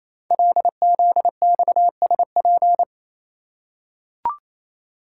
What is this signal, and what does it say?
Morse code, heard as a keyed tone, about 28 wpm, 700 Hz. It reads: LZXSP
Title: Code Groups - 5 Letters - 28wpm